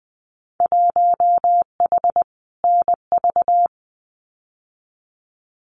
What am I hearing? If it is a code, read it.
Morse code, heard as a keyed tone, about 20 wpm, 700 Hz. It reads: 1H NV
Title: Winter Field Day Exchanges 20wpm